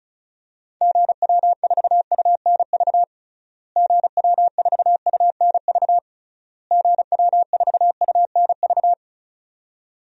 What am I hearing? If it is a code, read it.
Morse code, heard as a keyed tone, about 36 wpm, 700 Hz. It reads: GW4UNV GW4UNV GW4UNV
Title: DXCC-100 Call Signs - 6 Characters - Repeated 3x - 35wpm